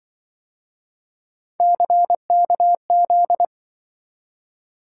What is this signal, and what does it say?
Morse code, heard as a keyed tone, about 24 wpm, 700 Hz. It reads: CKZ
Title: Speed-Racing - Three Letter-Number Combinations - 20wpm